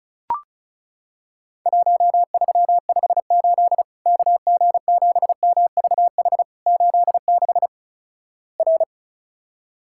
Morse code, heard as a keyed tone, about 36 wpm, 700 Hz. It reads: 1358 KG7MVH 86 R
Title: Worked All Europe DX Contest - QTC Exchange - 35wpm